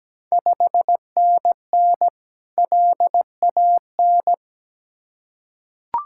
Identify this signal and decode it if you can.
Morse code, heard as a keyed tone, about 17 wpm, 700 Hz. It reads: 5NN LAN
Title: Ontario QSO Party Exchange 17wpm